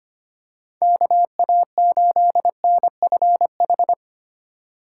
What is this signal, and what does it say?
Morse code, heard as a keyed tone, about 25 wpm, 700 Hz. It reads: KA8NFH
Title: USA Call Signs - 6 Characters - 4x Character Spacing - 25wpm